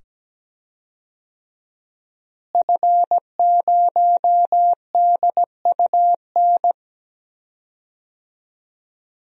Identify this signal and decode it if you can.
Morse code, heard as a keyed tone, about 17 wpm, 700 Hz. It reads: F0DUN
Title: DXCC-100 Call Signs - 5 Characters - Repeated 2x - 17wpm